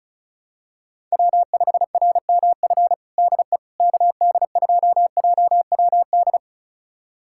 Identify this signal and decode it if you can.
Morse code, heard as a keyed tone, about 34 wpm, 700 Hz. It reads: W5RMF DE KD2JWD
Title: USA Callsign DE Callsign 35wpm